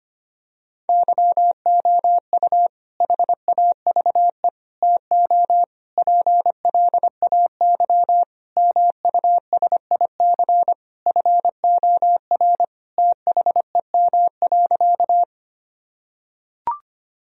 Morse code, heard as a keyed tone, about 25 wpm, 700 Hz. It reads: YOU HAVE TO PLAY MUSIC FOR THEM.
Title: Sentences from Top 1000 Words - Review Part 1 - 25wpm